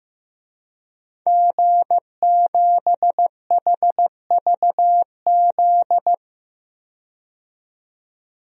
Morse code, heard as a keyed tone, about 16 wpm, 700 Hz. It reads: G7HVZ
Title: DXCC-100 Call Signs - 5 Characters - 15wpm